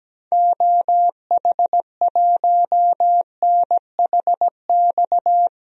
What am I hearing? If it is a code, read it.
Morse code, heard as a keyed tone, about 17 wpm, 700 Hz. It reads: OH1NHX
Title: DXCC-100 Call Signs - 6 Characters - Repeated 3x - 17wpm